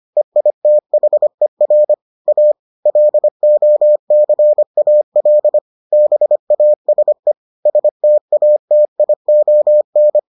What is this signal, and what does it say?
Morse code, heard as a keyed tone, about 25 wpm, 600 Hz. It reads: EITHER A LOCAL BASE STATION